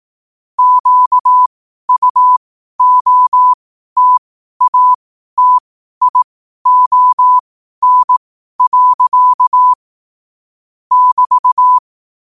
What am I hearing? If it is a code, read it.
Morse code, heard as a keyed tone, about 18 wpm, spaced as for 12 wpm, 1000 Hz. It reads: QUOTATION. <BT>
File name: qotdfiles / 12wpm